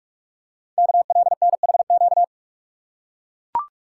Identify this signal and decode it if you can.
Morse code, heard as a keyed tone, about 44 wpm, 700 Hz. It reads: KLNHQ
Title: Code Groups - 5 Letters - 45wpm